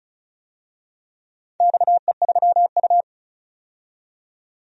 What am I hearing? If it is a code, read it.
Morse code, heard as a keyed tone, about 36 wpm, 700 Hz. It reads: XE3U
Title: DXCC-100 Call Signs - 4 Characters - 35wpm